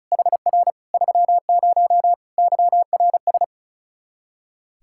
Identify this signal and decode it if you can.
Morse code, heard as a keyed tone, about 34 wpm, 700 Hz. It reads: HR 30 YRS
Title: Practice QSO - Exchange 3 - More Info - 35wpm